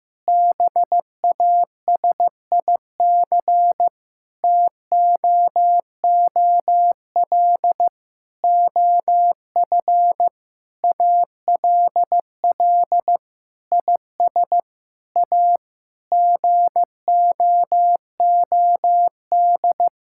Morse code, heard as a keyed tone, about 15 wpm, 700 Hz. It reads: BASIC TOOL OF ALL IS A GOOD
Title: Sentences from Top 1000 Words - Review Part 6 - 15wpm